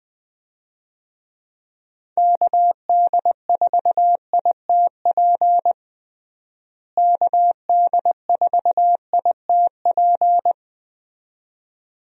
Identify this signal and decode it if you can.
Morse code, heard as a keyed tone, about 20 wpm, 700 Hz. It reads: KD4ITP KD4ITP
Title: USA Call Signs - 6 Characters - Repeated 2x - 20wpm